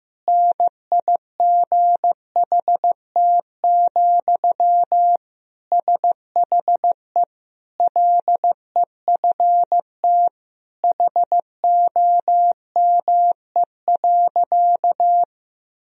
Morse code, heard as a keyed tone, about 15 wpm, 700 Hz. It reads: NIGHT, SHE LEFT HOME.
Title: Sentences from Top 1000 Words - Review Part 2 - 15wpm